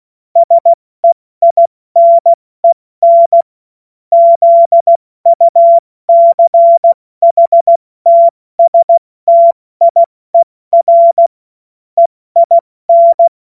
Morse code, heard as a keyed tone, about 16 wpm, 680 Hz. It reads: SEINEN ZUCHTSTIER EIN
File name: Morsekurs / Witze / Witz 10 Juni Mai 2025 70 F80